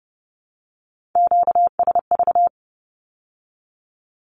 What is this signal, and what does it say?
Morse code, heard as a keyed tone, about 30 wpm, 700 Hz. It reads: QSV
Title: Q-Codes with Spoken Meaning - 30wpm